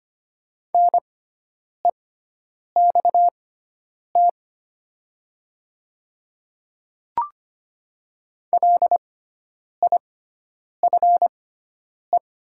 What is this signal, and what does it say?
Morse code, heard as a keyed tone, about 25 wpm, spaced as for 9 wpm, 700 Hz. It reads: NEXT LIFE
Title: 6x Character Spacing - Top 300 Words - 25wpm